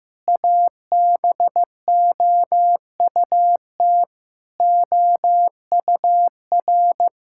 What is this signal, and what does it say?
Morse code, heard as a keyed tone, about 15 wpm, 700 Hz. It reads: ABOUT OUR
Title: Sentences from Top 1000 Words - Review Part 6 - 15wpm